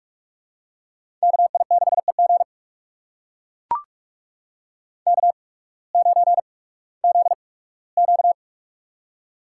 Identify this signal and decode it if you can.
Morse code, heard as a keyed tone, about 41 wpm, spaced as for 20 wpm, 700 Hz. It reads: KI6EG K9ZQ
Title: USA Call Signs - 8x Character Spacing - 45wpm